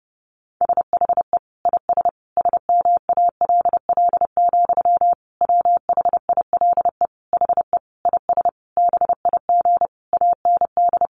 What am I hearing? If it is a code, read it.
Morse code, heard as a keyed tone, about 30 wpm, 700 Hz. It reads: SHE IS SMALL, WHILE HE IS BIG AND
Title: Sentences from Top 1300 Words 30wpm